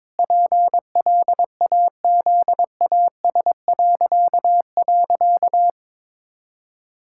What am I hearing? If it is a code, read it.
Morse code, heard as a keyed tone, about 22 wpm, 700 Hz. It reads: PLAZAS..
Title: Pangram Challenge v2 - 22wpm